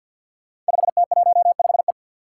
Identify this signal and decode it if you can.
Morse code, heard as a keyed tone, about 48 wpm, 700 Hz. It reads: 5T15E
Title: Five Challenging Letter-Number Combinations 50wpm